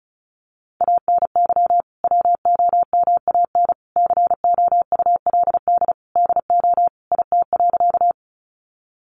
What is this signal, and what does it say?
Morse code, heard as a keyed tone, about 34 wpm, 700 Hz. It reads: ANY WOMAN COULD DO IT.
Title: Sentences from Top 1100 Words 35wpm